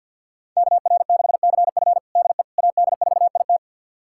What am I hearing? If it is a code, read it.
Morse code, heard as a keyed tone, about 49 wpm, 700 Hz. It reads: KR6CF DE AD4IT
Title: USA Callsign DE Callsign - 5 Characters - 50wpm